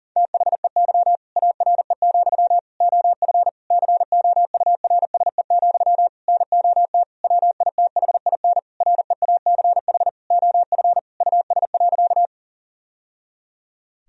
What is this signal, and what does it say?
Morse code, heard as a keyed tone, about 40 wpm, 700 Hz. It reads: THEY ARE, OF COURSE, NOT WITHIN REACH OF US.
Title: Sentences from Top 1000 Words - Review Part 4 - 40wpm